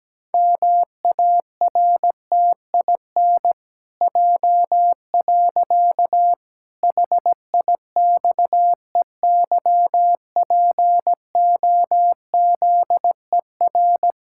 Morse code, heard as a keyed tone, about 17 wpm, 700 Hz. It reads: MARTIN J. HIXEYPOZER
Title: Pangram Challenge v2 - 17wpm